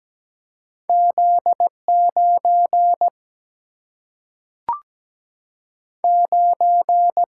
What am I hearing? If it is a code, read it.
Morse code, heard as a keyed tone, about 17 wpm, 700 Hz. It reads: Z 9 9
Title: Single-Letter Single-Number 17wpm